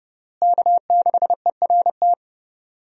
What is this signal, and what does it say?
Morse code, heard as a keyed tone, about 32 wpm, 700 Hz. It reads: K6ERT
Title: USA Call Signs - 5 Characters - Repeated 3x - 30wpm